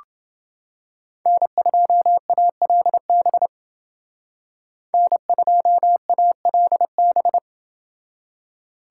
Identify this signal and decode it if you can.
Morse code, heard as a keyed tone, about 28 wpm, 700 Hz. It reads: N2ALB N2ALB